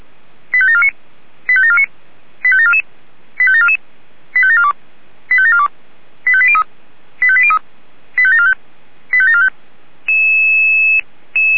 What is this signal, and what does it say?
Five-tone selective call (ZVEI), heard as quick runs of five tones, tones of about 70 ms. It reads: 85749 85749 85744 85744 85742 85742 85803 85803 85745 85745